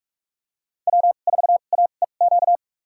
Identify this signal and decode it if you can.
Morse code, heard as a keyed tone, about 42 wpm, spaced as for 31 wpm, 700 Hz. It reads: W4AEQ
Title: USA Call Signs - 5 Characters - 2x Character Spacing - 45wpm